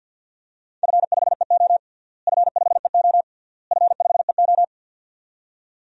Morse code, heard as a keyed tone, about 52 wpm, 700 Hz. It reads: F5EO F5EO F5EO
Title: DXCC-100 Call Signs - 4 Characters - Repeated 3x - 50wpm